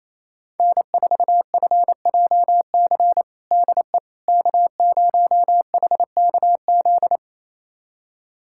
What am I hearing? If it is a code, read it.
Morse code, heard as a keyed tone, about 27 wpm, 700 Hz. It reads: N4FJC DE K0HKZ
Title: USA Callsign DE Callsign - 5 Characters - 28wpm